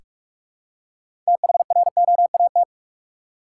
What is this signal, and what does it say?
Morse code, heard as a keyed tone, about 44 wpm, 700 Hz. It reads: THROAT
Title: Six Letter Words - 45wpm